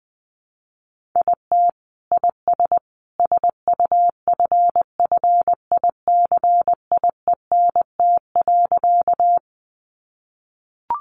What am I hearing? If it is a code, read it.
Morse code, heard as a keyed tone, about 20 wpm, 700 Hz. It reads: IT IS SUFFICIENT.
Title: Sentences from Top 1400 Words 20wpm